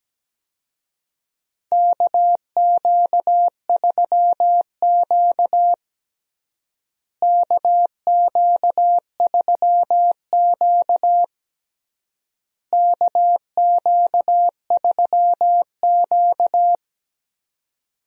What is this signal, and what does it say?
Morse code, heard as a keyed tone, about 17 wpm, 700 Hz. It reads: KQ3Q KQ3Q KQ3Q
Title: USA Call Signs - 4 Characters - Repeated 3x - 17wpm